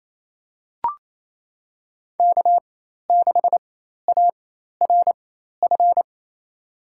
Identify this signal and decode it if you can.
Morse code, heard as a keyed tone, about 28 wpm, spaced as for 13 wpm, 700 Hz. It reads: K6ARF
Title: USA Call Signs - 5 Characters - 4x Character Spacing - 28wpm